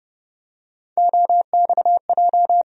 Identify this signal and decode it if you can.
Morse code, heard as a keyed tone, about 30 wpm, 700 Hz. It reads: OXJ
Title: USA Call Sign Postfixes - 3 Letters 30wpm